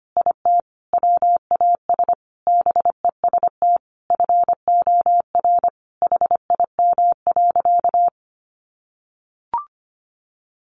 Morse code, heard as a keyed tone, about 25 wpm, 700 Hz. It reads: IT WAS BEST FOR HIM.
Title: Sentences from Top 500 Words 25wpm